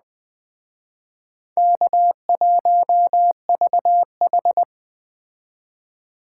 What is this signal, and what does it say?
Morse code, heard as a keyed tone, about 20 wpm, 700 Hz. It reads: K1VH